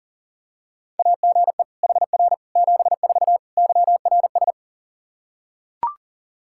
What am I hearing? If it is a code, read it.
Morse code, heard as a keyed tone, about 40 wpm, 700 Hz. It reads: AGE HR 74 YRS
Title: Practice QSO - Exchange 3 - More Info - 40wpm